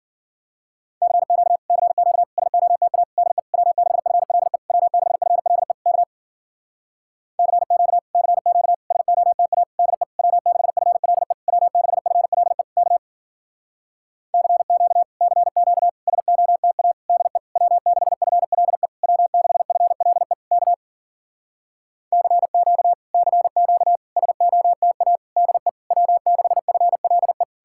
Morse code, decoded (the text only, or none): CQ CQ SOTA DE W6FLE W6FLE K CQ CQ SOTA DE W6FLE W6FLE K CQ CQ SOTA DE W6FLE W6FLE K CQ CQ SOTA DE W6FLE